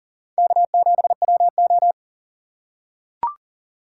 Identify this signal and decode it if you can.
Morse code, heard as a keyed tone, about 39 wpm, 700 Hz. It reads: K7WO
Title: USA Call Signs - 3x Character Spacing - 40wpm